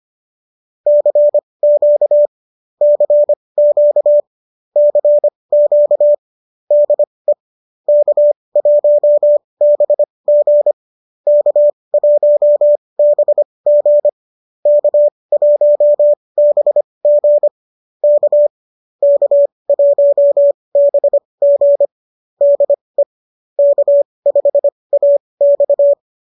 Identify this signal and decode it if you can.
Morse code, heard as a keyed tone, about 25 wpm, spaced as for 20 wpm, 590 Hz. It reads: CQ CQ CQ DE K1BG K1BG K1BG K K1BG DE K5AX
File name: qso201 20